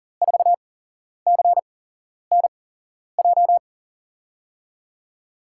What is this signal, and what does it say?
Morse code, heard as a keyed tone, about 39 wpm, spaced as for 11 wpm, 700 Hz. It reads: 4CNJ
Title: USA Call Signs - 8x Character Spacing - 40wpm